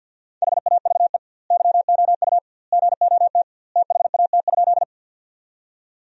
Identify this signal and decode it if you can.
Morse code, heard as a keyed tone, about 47 wpm, 700 Hz. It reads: HAVE YOU GOT THAT?
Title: 2x Word-Spacing - Sentences from Top 300 Words - 50wpm